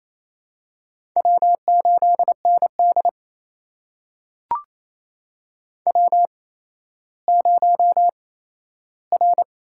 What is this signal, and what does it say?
Morse code, heard as a keyed tone, about 28 wpm, spaced as for 14 wpm, 700 Hz. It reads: W8ND W0R
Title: USA Call Signs - 8x Character Spacing - 28wpm